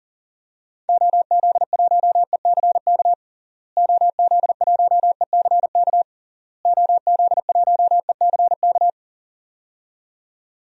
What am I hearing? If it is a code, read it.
Morse code, heard as a keyed tone, about 40 wpm, 700 Hz. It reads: OZ1ECK OZ1ECK OZ1ECK
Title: DXCC-100 Call Signs - 6 Characters - Repeated 3x - 40wpm